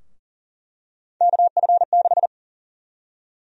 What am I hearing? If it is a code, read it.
Morse code, heard as a keyed tone, about 40 wpm, 700 Hz. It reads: KF6
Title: USA Call Sign Prefixes - Repeated 2x - 40wpm